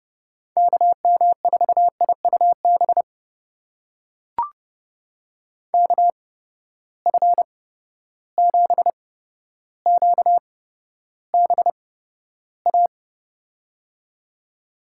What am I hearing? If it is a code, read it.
Morse code, heard as a keyed tone, about 30 wpm, spaced as for 13 wpm, 700 Hz. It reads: KM4IUB KF7QBA
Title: USA Call Signs - 6 Characters - 8x Character Spacing - 30wpm